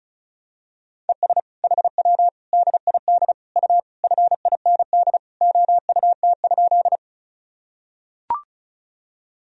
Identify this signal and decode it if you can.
Morse code, heard as a keyed tone, about 34 wpm, 700 Hz. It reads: ES HW DID U FIND OUT?
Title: Sentences from Top 200 Words and Abbreviations 35wpm